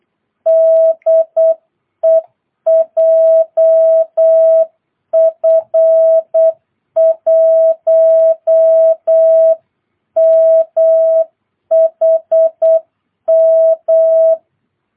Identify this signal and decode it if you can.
Morse code, heard as a keyed tone, about 8 wpm, 660 Hz. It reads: DE JF1MHM
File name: jh8cbh / 2022cw